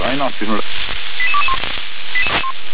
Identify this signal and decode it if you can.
Five-tone selective call (ZVEI), heard as quick runs of five tones, tones of about 70 ms, with speech. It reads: 88221